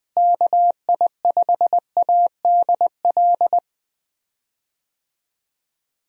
Morse code, heard as a keyed tone, about 19 wpm, 700 Hz. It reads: KI5ADL
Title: USA Call Signs - 6 Characters - Repeated 3x - 20wpm